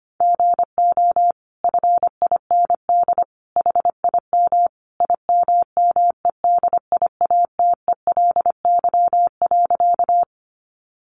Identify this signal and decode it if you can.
Morse code, heard as a keyed tone, about 25 wpm, 700 Hz. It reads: GO FIND HIM IMMEDIATELY.
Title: Sentences from Top 1100 Words 25wpm